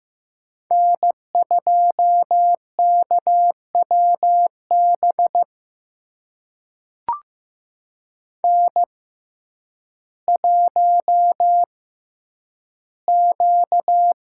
Morse code, heard as a keyed tone, about 15 wpm, spaced as for 9 wpm, 700 Hz. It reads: N2KWB N1Q
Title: USA Call Signs - 6x Character Spacing - 15wpm